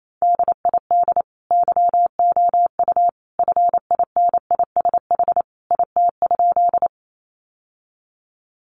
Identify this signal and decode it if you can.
Morse code, heard as a keyed tone, about 28 wpm, 700 Hz. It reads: DID YOU FINISH IT?
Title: Sentences from Top 1200 Words 28wpm